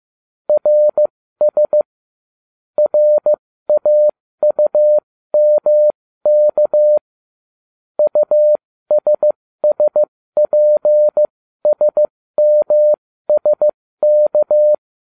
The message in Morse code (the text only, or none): RS RAUMK USSPSMSK